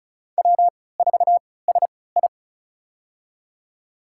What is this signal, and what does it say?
Morse code, heard as a keyed tone, about 34 wpm, spaced as for 20 wpm, 700 Hz. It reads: W4SI
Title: USA Call Signs - 3x Character Spacing - 35wpm